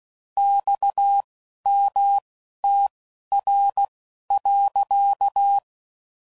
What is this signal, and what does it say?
Morse code, heard as a keyed tone, about 16 wpm, spaced as for 11 wpm, 800 Hz. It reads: XMTR.
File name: CW Code Groups @10WPM / QSOs / QSO WA7QOX